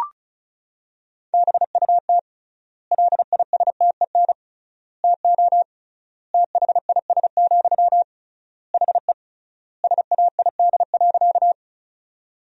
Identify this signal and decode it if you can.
Morse code, heard as a keyed tone, about 36 wpm, 700 Hz. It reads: BUT LISTEN TO THIS, HE SAID.
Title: 3x Word-Spacing - Sentences from Top 400 Words - 35wpm